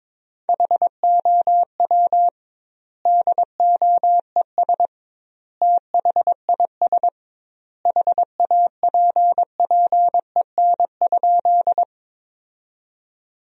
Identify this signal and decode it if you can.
Morse code, heard as a keyed tone, about 22 wpm, 700 Hz. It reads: HOW DOES THIS HAPPEN?
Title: 2x Word-Spacing - Sentences from Top 400 Words - 22wpm